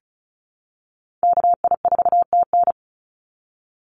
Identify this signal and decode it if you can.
Morse code, heard as a keyed tone, about 36 wpm, 700 Hz. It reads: KI4TN
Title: NA SOTA Activator Callsigns 35wpm